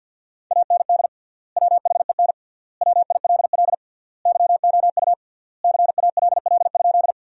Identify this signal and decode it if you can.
Morse code, heard as a keyed tone, about 50 wpm, 700 Hz. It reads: AND WHEN WILL YOU CALL?
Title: 3x Word-Spacing - Sentences from Top 100 Words - 50wpm